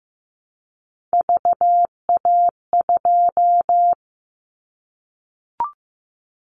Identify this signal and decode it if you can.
Morse code, heard as a keyed tone, about 15 wpm, 700 Hz. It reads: VA2
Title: Callsign Prefixes for Top 100 DXCC - 15wpm